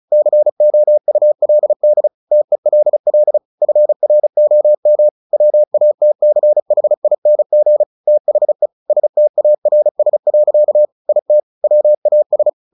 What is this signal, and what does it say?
Morse code, heard as a keyed tone, about 34 wpm, 600 Hz. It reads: COULD TELL FROM WATCHING THE STARS. IT WAS
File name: the old man and the sea / the old man and the sea 1 35wpm